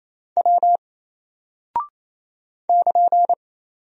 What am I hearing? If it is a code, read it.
Morse code, heard as a keyed tone, about 27 wpm, 700 Hz. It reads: W <KN>
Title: Single Character (Letter:Number:Punctuation:Prosign) 28wpm